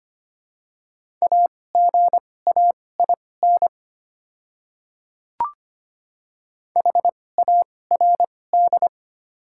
Morse code, heard as a keyed tone, about 25 wpm, spaced as for 18 wpm, 700 Hz. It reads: AGAIN HARD
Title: 2x Character Spacing - Top 300 Words - 25wpm